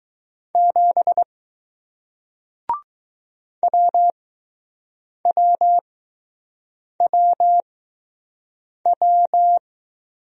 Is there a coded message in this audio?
Yes: Morse code from 0.6 to 9.6 s, heard as a keyed tone, about 20 wpm, spaced as for 6 wpm, 700 Hz. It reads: 7 WWWW